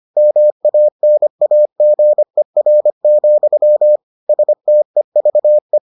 Morse code, heard as a keyed tone, about 25 wpm, 600 Hz. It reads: MANAGER, STEVE